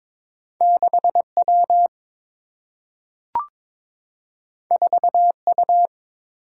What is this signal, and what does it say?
Morse code, heard as a keyed tone, about 22 wpm, 700 Hz. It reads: 6 W 4 U
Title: Single-Letter Single-Number 22wpm